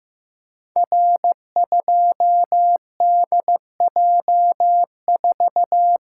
Morse code, heard as a keyed tone, about 15 wpm, 700 Hz. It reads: R2DJ4
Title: Code Groups - 5 Letter:Numbers - 15wpm